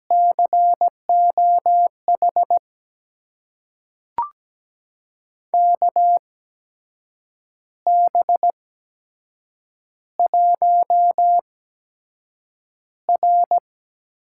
Morse code, heard as a keyed tone, about 17 wpm, spaced as for 7 wpm, 700 Hz. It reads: COH KB1R
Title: USA Call Signs - 6 Characters - 8x Character Spacing - 17wpm